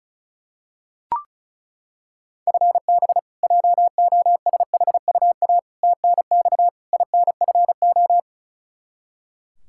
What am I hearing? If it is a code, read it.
Morse code, heard as a keyed tone, about 34 wpm, 700 Hz. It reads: FB JOSHUA TNX INFO
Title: K1USN SST QSO Practice 35wpm